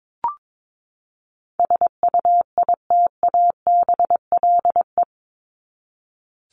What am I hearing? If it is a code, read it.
Morse code, heard as a keyed tone, about 22 wpm, 700 Hz. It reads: SUITABLE
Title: Top 1251-1500 Words 22wpm